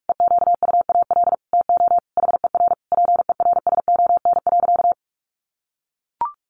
Morse code, heard as a keyed tone, about 44 wpm, 700 Hz. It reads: EQUAL TO HER PERSON.
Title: Sentences from Top 1400 Words 45wpm